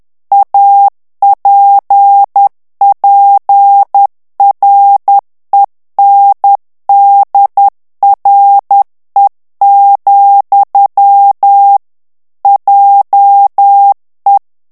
Morse code, heard as a keyed tone, about 10 wpm, 800 Hz. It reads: APPRENDRE, JE
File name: morse mp3 / FRANCAIS TEXTE CLAIR 2 VITESSE 10 MOTS MINUTES